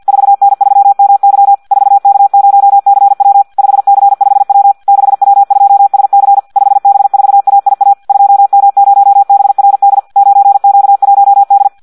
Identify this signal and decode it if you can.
Morse code, heard as a keyed tone, about 50 wpm, 800 Hz. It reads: 4N3MY 4K0CW 5C5W 6W2SC 5B4AIA 2M0BRD 9Q1D
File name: Audio practica CW / qrq-tx-50 2